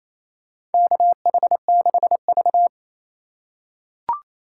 Morse code, heard as a keyed tone, about 28 wpm, 700 Hz. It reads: KH6V
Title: USA Call Signs - 4 Characters - 3x Character Spacing - 28wpm